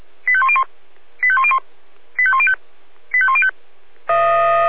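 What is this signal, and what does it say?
Five-tone selective call (ZVEI), heard as quick runs of five tones, tones of about 70 ms. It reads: 85101 85101 85105 85105
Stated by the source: ZVEI five-tone call 85102 received on SDR.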